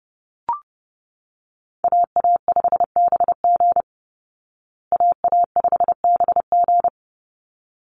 Morse code, heard as a keyed tone, about 30 wpm, 700 Hz. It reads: AA5BG AA5BG
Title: Call Signs Repeated 3x - 30wpm